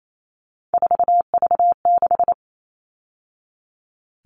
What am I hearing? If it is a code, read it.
Morse code, heard as a keyed tone, about 28 wpm, 700 Hz. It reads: 4V6
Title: All Cut Numbers - 3-Digit 28wpm